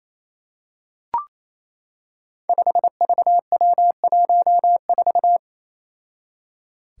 Morse code, heard as a keyed tone, about 28 wpm, 700 Hz. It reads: 5VW14
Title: Five Challenging Letter-Number Combinations 28wpm